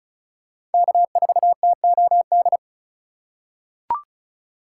Morse code, heard as a keyed tone, about 36 wpm, 700 Hz. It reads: K4TOD
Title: USA Call Signs - 5 Characters - 3x Character Spacing - 35wpm